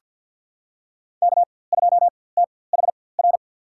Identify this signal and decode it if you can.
Morse code, heard as a keyed tone, about 50 wpm, spaced as for 23 wpm, 700 Hz. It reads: K2THR
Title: USA Call Signs - 5 Characters - 4x Character Spacing - 50wpm